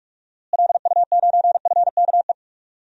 Morse code, heard as a keyed tone, about 46 wpm, 700 Hz. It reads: LU9FKE
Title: DXCC-100 Call Signs - 6 Characters - Repeated 3x - 45wpm